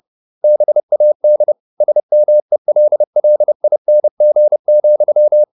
Morse code, heard as a keyed tone, about 30 wpm, 600 Hz. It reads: BAD SMELLING,